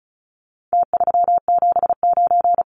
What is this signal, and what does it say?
Morse code, heard as a keyed tone, about 34 wpm, 700 Hz. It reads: T379
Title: Numbers - Contest Style - 4-Digit 35wpm